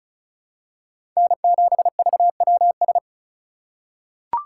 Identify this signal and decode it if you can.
Morse code, heard as a keyed tone, about 36 wpm, 700 Hz. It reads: N7VWS
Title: USA Call Signs - 5 Characters - 6x Character Spacing - 35wpm